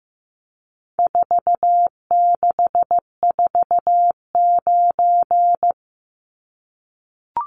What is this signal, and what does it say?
Morse code, heard as a keyed tone, about 15 wpm, 700 Hz. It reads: 4649
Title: Numbers - 4-Digit 15wpm